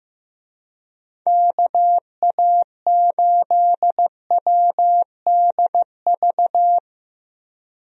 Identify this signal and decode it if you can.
Morse code, heard as a keyed tone, about 15 wpm, 700 Hz. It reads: KA8WDV
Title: USA Call Signs - 6 Characters - Repeated 3x - 15wpm